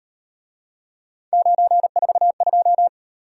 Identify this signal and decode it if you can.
Morse code, heard as a keyed tone, about 38 wpm, 700 Hz. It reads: 942